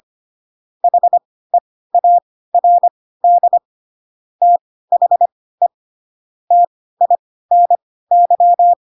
Morse code, heard as a keyed tone, about 25 wpm, spaced as for 16 wpm, 700 Hz. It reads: HEARD THE TINY
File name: Pinocchio / Pinocchio ch1 10 25 ik1hge